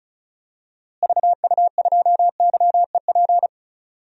Morse code, heard as a keyed tone, about 36 wpm, 700 Hz. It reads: VU2YEP
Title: DXCC-100 Call Signs - 6 Characters - Repeated 2x - 35wpm